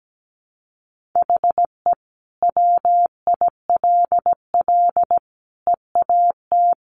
Morse code, heard as a keyed tone, about 17 wpm, 700 Hz. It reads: HE WILL EAT
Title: Sentences from Top 1000 Words 17wpm